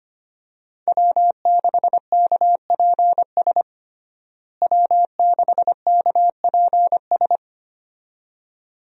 Morse code, heard as a keyed tone, about 25 wpm, 700 Hz. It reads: W6KPS W6KPS
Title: USA Call Signs - 5 Characters - Repeated 2x - 25wpm